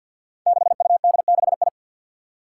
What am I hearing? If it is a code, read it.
Morse code, heard as a keyed tone, about 48 wpm, 700 Hz. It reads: 6UD6I
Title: Five Challenging Letter-Number Combinations 50wpm